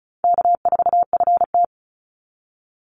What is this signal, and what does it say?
Morse code, heard as a keyed tone, about 34 wpm, 700 Hz. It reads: K4FT
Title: Call Signs - 35wpm